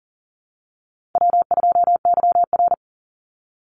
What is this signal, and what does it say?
Morse code, heard as a keyed tone, about 37 wpm, 700 Hz. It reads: W2YR
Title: Call Signs - 40wpm